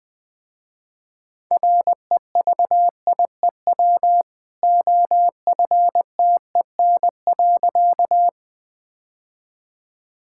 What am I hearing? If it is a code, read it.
Morse code, heard as a keyed tone, about 20 wpm, 700 Hz. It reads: REVIEW OFTEN.
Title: Easy Sentences with Hints - 20wpm